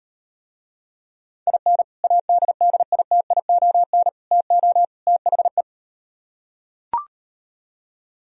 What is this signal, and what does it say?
Morse code, heard as a keyed tone, about 38 wpm, 700 Hz. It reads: IN ADDITION TO THE